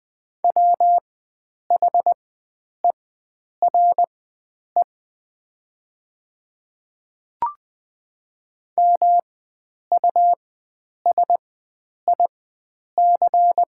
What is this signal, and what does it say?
Morse code, heard as a keyed tone, about 20 wpm, spaced as for 9 wpm, 700 Hz. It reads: WHERE MUSIC
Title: 4x Character Spacing - Top 300 Words - 20wpm